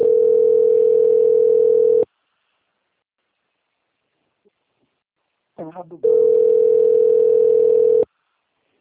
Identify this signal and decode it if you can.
Ringback tone (North America), 440 + 480 Hz, with speech.
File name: out- - - - - .19052